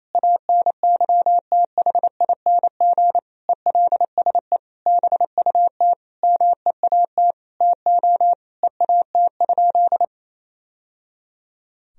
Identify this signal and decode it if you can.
Morse code, heard as a keyed tone, about 28 wpm, 700 Hz. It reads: ANYTHING ELSE BUT MEAT TO EAT?
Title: Sentences from Top 1000 Words - Review Part 3 - 28wpm